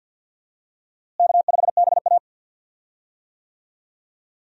Morse code, heard as a keyed tone, about 50 wpm, 700 Hz. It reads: K5BA
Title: USA Call Signs - 4 Characters - Repeated 2x - 50wpm